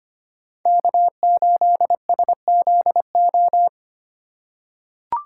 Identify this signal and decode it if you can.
Morse code, heard as a keyed tone, about 25 wpm, 700 Hz. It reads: K8SZO
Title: USA Call Signs - 5 Characters - 4x Character Spacing - 25wpm